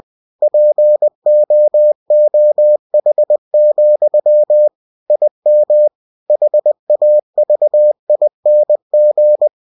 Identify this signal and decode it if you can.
Morse code, heard as a keyed tone, about 20 wpm, 600 Hz. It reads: POOH, IM HAVING
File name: cw-books / winnie-the-pooh / winnie-20wpm-page-0057